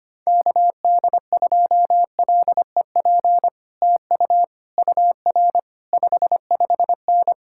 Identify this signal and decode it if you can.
Morse code, heard as a keyed tone, about 24 wpm, 700 Hz. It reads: KD2LEP TU UR 55N
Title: Speed-Racing Common QSO Phrases Encore - 25wpm